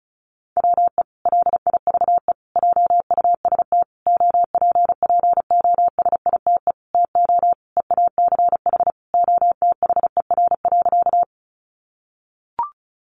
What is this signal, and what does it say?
Morse code, heard as a keyed tone, about 34 wpm, 700 Hz. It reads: WE LIVE JUST OPPOSITE TO EACH OTHER.
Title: Sentences from Top 1300 Words 35wpm